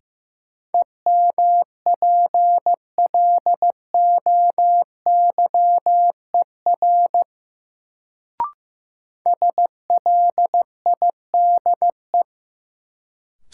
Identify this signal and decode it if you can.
Morse code, heard as a keyed tone, about 15 wpm, 700 Hz. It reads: EMPLOYER SLIDE
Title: Sentences from Top 1600 Words - New Word Drill Alt - 15wpm